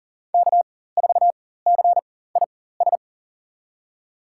Morse code, heard as a keyed tone, about 40 wpm, spaced as for 19 wpm, 700 Hz. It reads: K4CIS
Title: USA Call Signs - 5 Characters - 4x Character Spacing - 40wpm